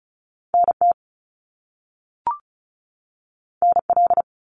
Morse code, heard as a keyed tone, about 36 wpm, 700 Hz. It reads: NT NL
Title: Canadian Provinces and Territories 35wpm